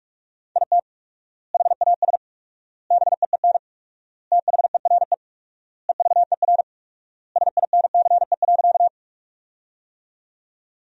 Morse code, heard as a keyed tone, about 41 wpm, 700 Hz. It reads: IT HAS BEEN THERE EVER SINCE.
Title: 4x Word-Spacing - Sentences from Top 400 Words - 45wpm